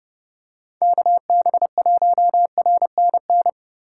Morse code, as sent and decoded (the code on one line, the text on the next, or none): -.- -... .---- .-. -. -.
KB1RNN